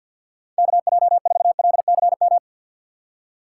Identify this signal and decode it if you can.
Morse code, heard as a keyed tone, about 51 wpm, 690 Hz. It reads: KJ4LCM
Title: USA Call Signs - 6 Characters - 3x Character Spacing - 50wpm